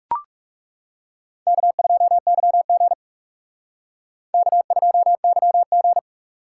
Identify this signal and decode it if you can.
Morse code, heard as a keyed tone, about 37 wpm, 700 Hz. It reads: K2YG K2YG
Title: Speed Racing - USA 1x2 Call Signs 30wpm